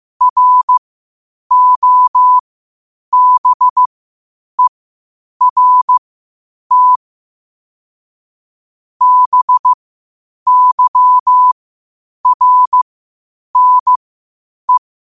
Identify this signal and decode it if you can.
Morse code, heard as a keyed tone, about 15 wpm, spaced as for 8 wpm, 1000 Hz. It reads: ROBERT BYRNE
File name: qotdfiles / 8wpm